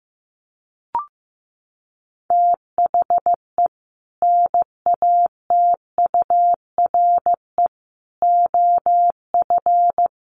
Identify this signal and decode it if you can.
Morse code, heard as a keyed tone, about 14 wpm, 700 Hz. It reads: THE NATURE OF
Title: Sets of 4 Words 15wpm